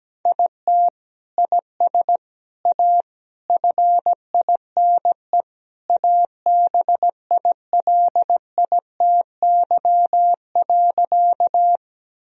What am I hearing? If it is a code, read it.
Morse code, heard as a keyed tone, about 17 wpm, 700 Hz. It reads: IT IS A FINE ABILITY.
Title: Sentences from Top 1000 Words - Review Part 3 - 17wpm